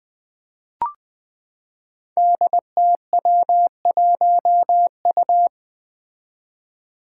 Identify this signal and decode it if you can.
Morse code, heard as a keyed tone, about 20 wpm, 700 Hz. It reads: DTW1U
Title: Five Challenging Letter-Number Combinations 20wpm